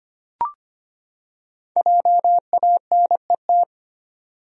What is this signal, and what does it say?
Morse code, heard as a keyed tone, about 25 wpm, 700 Hz. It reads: JANET
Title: USA Common Names - 25wpm